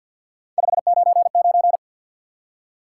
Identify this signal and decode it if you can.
Morse code, heard as a keyed tone, about 50 wpm, 700 Hz. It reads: 599
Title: Top 180 QSO Elements 50wpm